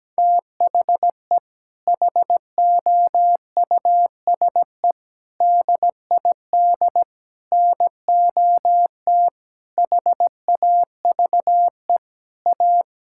Morse code, heard as a keyed tone, about 17 wpm, 700 Hz. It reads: THE HOUSE DID NOT HAVE A
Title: Sentences from Top 1600 Words 17wpm